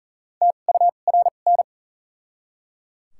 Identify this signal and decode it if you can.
Morse code, heard as a keyed tone, about 38 wpm, spaced as for 28 wpm, 700 Hz. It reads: TURN
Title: 2x Character Spacing - Top 500 Words - 40wpm